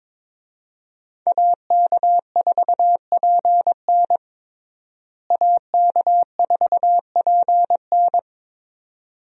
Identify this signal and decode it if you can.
Morse code, heard as a keyed tone, about 21 wpm, 700 Hz. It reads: AK4PN AK4PN
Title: USA Call Signs - 5 Characters - Repeated 2x - 22wpm